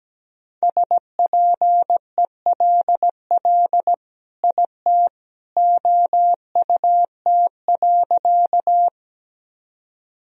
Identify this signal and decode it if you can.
Morse code, heard as a keyed tone, about 17 wpm, 700 Hz. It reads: SPELL IT OUT.
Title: Sentences from Top 1000 Words - Review Part 1 - 17wpm